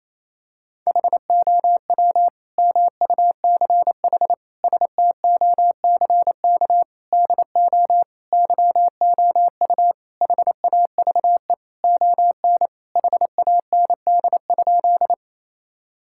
Morse code, heard as a keyed tone, about 28 wpm, 700 Hz. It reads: HOW MUCH STOCK DO YOU HAVE ON HAND?
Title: Sentences from Top 1000 Words - Review Part 5 - 28wpm